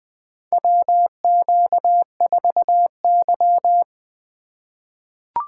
Morse code, heard as a keyed tone, about 20 wpm, 700 Hz. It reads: WQ4Y
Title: USA Call Signs - 3x Character Spacing - 20wpm